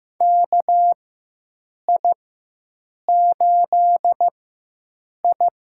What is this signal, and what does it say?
Morse code, heard as a keyed tone, about 15 wpm, spaced as for 7 wpm, 700 Hz. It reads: KI8I
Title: USA Call Signs - 4x Character Spacing - 15wpm